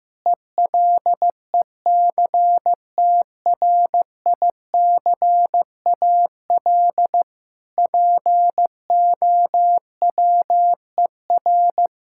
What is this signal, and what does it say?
Morse code, heard as a keyed tone, about 15 wpm, 700 Hz. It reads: ELECTRICAL POWER
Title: Sentences from Top 1000 Words - Review Part 6 - 15wpm